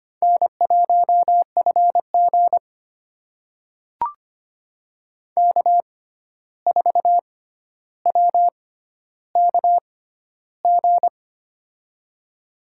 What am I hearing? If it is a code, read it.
Morse code, heard as a keyed tone, about 25 wpm, spaced as for 12 wpm, 700 Hz. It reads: N1FG K4WKG
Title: USA Call Signs - 6x Character Spacing - 25wpm